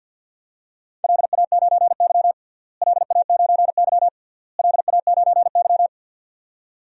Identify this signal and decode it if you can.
Morse code, heard as a keyed tone, about 48 wpm, 700 Hz. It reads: LA9Y LA9Y LA9Y
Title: DXCC-100 Call Signs - 4 Characters - Repeated 3x - 50wpm